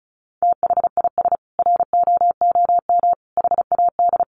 Morse code, heard as a keyed tone, about 34 wpm, 700 Hz. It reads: THIS ROOM HAD
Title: Sentences from Top 700 Words 35wpm